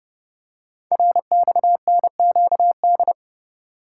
Morse code, heard as a keyed tone, about 30 wpm, 700 Hz. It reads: RXNQD
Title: Code Groups - 5 Letters - 30wpm